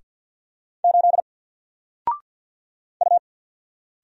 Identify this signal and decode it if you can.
Morse code, heard as a keyed tone, about 50 wpm, 700 Hz. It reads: 8 U
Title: Single Letter-Number Alt 50wpm